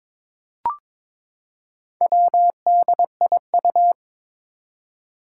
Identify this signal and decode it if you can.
Morse code, heard as a keyed tone, about 23 wpm, 700 Hz. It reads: WDIU
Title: Four Challenging Letter-Number Combinations 22wpm